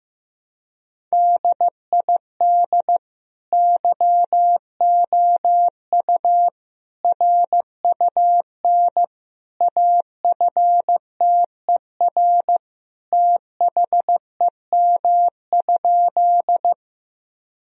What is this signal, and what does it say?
Morse code, heard as a keyed tone, about 15 wpm, 700 Hz. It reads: DID YOU RUN AFTER THEM?
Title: Sentences from Top 1000 Words - Review Part 1 - 15wpm